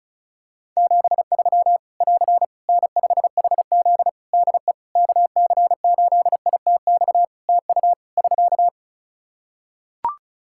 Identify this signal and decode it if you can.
Morse code, heard as a keyed tone, about 36 wpm, 700 Hz. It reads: 73 <AR> N5HZ DE KC8ITX TU <SK>
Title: Practice QSO - Exchange 4 - Ending - 35wpm